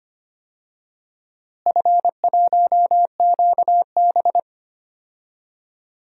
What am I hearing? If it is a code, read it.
Morse code, heard as a keyed tone, about 25 wpm, 700 Hz. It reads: F1QB
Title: DXCC-100 Call Signs - 4 Characters - Repeated 2x - 25wpm